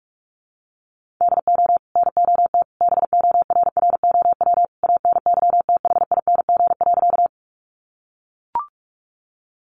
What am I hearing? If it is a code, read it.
Morse code, heard as a keyed tone, about 44 wpm, 700 Hz. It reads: DO NOT BORROW ANYTHING.
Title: Sentences from Top 1500 Words 45wpm